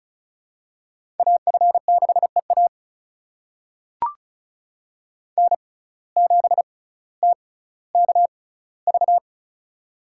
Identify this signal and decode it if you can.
Morse code, heard as a keyed tone, about 36 wpm, spaced as for 18 wpm, 700 Hz. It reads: AF6EA N7TKV
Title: USA Call Signs - 5 Characters - 6x Character Spacing - 35wpm